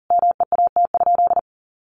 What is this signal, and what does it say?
Morse code, heard as a keyed tone, about 40 wpm, 700 Hz. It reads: MEAT?
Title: Sentences from Top 1500 Words 40wpm